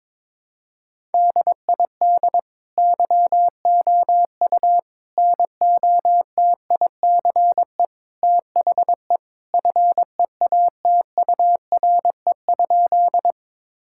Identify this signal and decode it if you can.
Morse code, heard as a keyed tone, about 22 wpm, 700 Hz. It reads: DID YOU NOTICE THE FEATURE?
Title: Sentences from Top 1000 Words - Review Part 5 - 22wpm